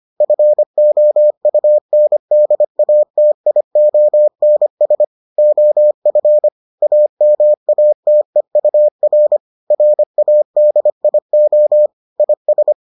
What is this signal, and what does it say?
Morse code, heard as a keyed tone, about 25 wpm, 600 Hz. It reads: FOUNDATIONS OF AMATEUR RADIO IS